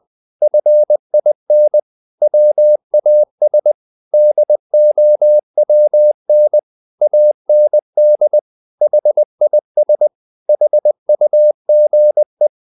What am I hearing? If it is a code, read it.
Morse code, heard as a keyed tone, about 20 wpm, 600 Hz. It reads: FIN WAS DOWN AND HIS HUGE
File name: the old man and the sea part 2 20wpm